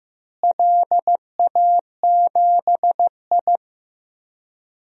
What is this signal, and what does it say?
Morse code, heard as a keyed tone, about 15 wpm, 700 Hz. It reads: LA7I
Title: DXCC-100 Call Signs - 4 Characters - Repeated 3x - 15wpm